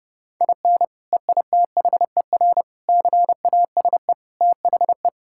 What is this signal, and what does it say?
Morse code, heard as a keyed tone, about 32 wpm, 700 Hz. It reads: IN EITHER CASE THE
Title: Sentences from Top 1000 Words - Review Part 6 - 30wpm